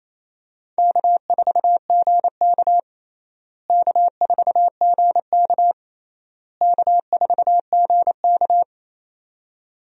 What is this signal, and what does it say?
Morse code, heard as a keyed tone, about 28 wpm, 700 Hz. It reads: K4GK K4GK K4GK
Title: USA Call Signs - 4 Characters - Repeated 3x - 28wpm